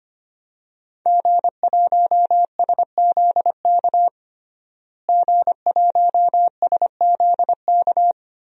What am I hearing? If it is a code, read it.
Morse code, heard as a keyed tone, about 25 wpm, 700 Hz. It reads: G1SZK G1SZK
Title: DXCC-100 Call Signs - 5 Characters - Repeated 2x - 25wpm